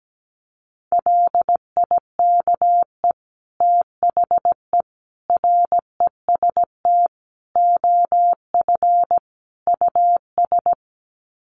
Morse code, heard as a keyed tone, about 17 wpm, 700 Hz. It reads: LIKE THE REST OF US
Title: Sets of 5 Words Encore2 17wpm